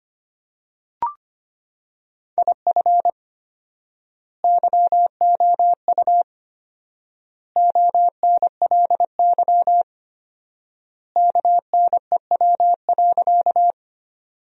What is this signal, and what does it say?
Morse code, heard as a keyed tone, about 25 wpm, 700 Hz. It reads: IF YOU ONLY KNEW.
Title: 4x Word-Spacing - Sentences from Top 400 Words - 25wpm